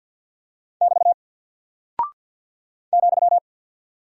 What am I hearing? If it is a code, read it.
Morse code, heard as a keyed tone, about 48 wpm, 700 Hz. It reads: <BT> ,
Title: Single Character (Letter:Number:Punctuation:Prosign) 50wpm